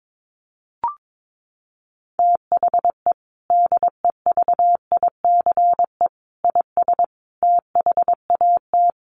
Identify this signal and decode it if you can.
Morse code, heard as a keyed tone, about 22 wpm, 700 Hz. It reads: THE DEVICE IS THAT
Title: Sentences from Top 1500 Words 22wpm